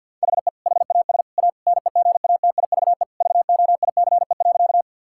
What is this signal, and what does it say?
Morse code, heard as a keyed tone, about 50 wpm, 700 Hz. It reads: HE HAS A NEGATIVE VOICE.
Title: Sentences from Top 1000 Words - Review Part 6 - 50wpm